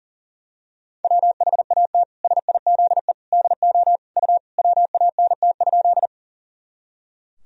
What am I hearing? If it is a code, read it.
Morse code, heard as a keyed tone, about 40 wpm, 700 Hz. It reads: WHAT SIZE DO U WANT?
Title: Sentences from Top 400 Words and Abbreviations 40wpm